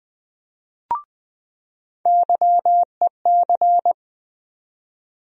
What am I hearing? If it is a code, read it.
Morse code, heard as a keyed tone, about 20 wpm, 700 Hz. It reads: YEC
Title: USA Call Sign Postfixes - 3 Letters 20wpm